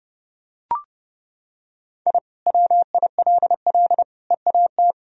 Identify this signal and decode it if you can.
Morse code, heard as a keyed tone, about 30 wpm, 700 Hz. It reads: I WILL EAT
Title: Sentences from Top 1000 Words - Review Part 1 - 30wpm